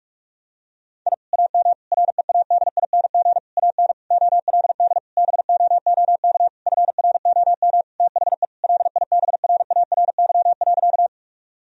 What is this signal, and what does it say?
Morse code, heard as a keyed tone, about 46 wpm, 700 Hz. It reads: I AM READING AN OLD BOOK FROM THE LIBRARY.
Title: Sentences from Top 1000 Words - Review Part 4 - 45wpm